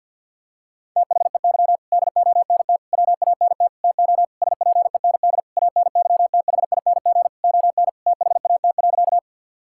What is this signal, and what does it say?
Morse code, heard as a keyed tone, about 52 wpm, 700 Hz. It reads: THEY DONT WANT TO SPEND ANYTHING ON THAT.
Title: Sentences from Top 1000 Words - Review Part 5 - 50wpm